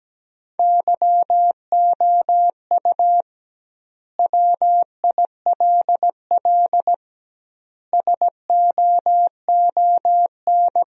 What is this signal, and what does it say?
Morse code, heard as a keyed tone, about 17 wpm, 700 Hz. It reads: YOU WILL SOON
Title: 2x Word-Spacing - Sentences from Top 400 Words - 17wpm